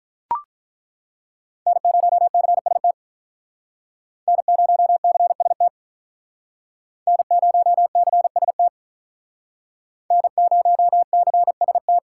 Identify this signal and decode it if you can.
Morse code, heard as a keyed tone, about 48 wpm, 700 Hz. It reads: N0CST N0CST N0CST N0CST